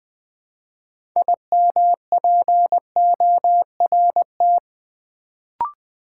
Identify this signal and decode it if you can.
Morse code, heard as a keyed tone, about 20 wpm, 700 Hz. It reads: IMPORT
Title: Build A Word - 3 Letters At A Time - 6-Letter Words - 20wpm